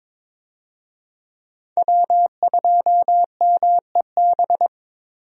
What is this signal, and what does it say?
Morse code, heard as a keyed tone, about 22 wpm, 700 Hz. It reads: W2MEB
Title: USA Call Signs - 5 Characters - Repeated 2x - 22wpm